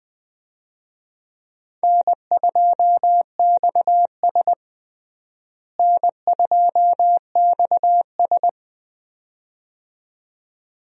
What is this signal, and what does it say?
Morse code, heard as a keyed tone, about 21 wpm, 700 Hz. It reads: N2XS N2XS
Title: USA Call Signs - 4 Characters - Repeated 2x - 20wpm